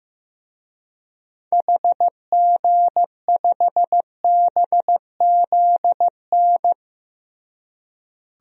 Morse code, heard as a keyed tone, about 16 wpm, 700 Hz. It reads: HG5BZN
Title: DXCC-100 Call Signs - 6 Characters - Repeated 3x - 15wpm